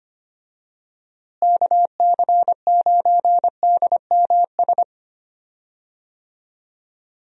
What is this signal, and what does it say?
Morse code, heard as a keyed tone, about 25 wpm, 700 Hz. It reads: KC9DMS
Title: USA Call Signs - 6 Characters - Repeated 3x - 25wpm